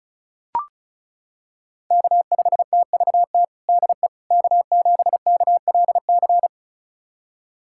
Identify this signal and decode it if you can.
Morse code, heard as a keyed tone, about 36 wpm, 700 Hz. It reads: K5TVT DE K7KLC
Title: USA Callsign DE Callsign - 5 Characters - 35wpm